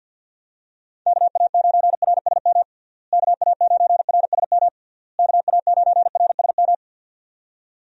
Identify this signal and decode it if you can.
Morse code, heard as a keyed tone, about 52 wpm, 700 Hz. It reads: KA9RSM KA9RSM KA9RSM
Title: USA Call Signs - 6 Characters - Repeated 3x - 50wpm